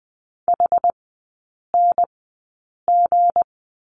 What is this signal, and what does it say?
Morse code, heard as a keyed tone, about 20 wpm, spaced as for 8 wpm, 690 Hz. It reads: HNG
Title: 159 Lesson 26e - Letter g - Three Characters 20wpm